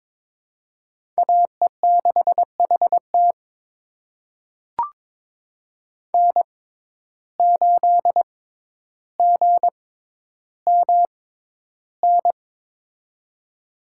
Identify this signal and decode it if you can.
Morse code, heard as a keyed tone, about 22 wpm, spaced as for 11 wpm, 700 Hz. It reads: AE6HT N8GMN
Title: USA Call Signs - 5 Characters - 6x Character Spacing - 22wpm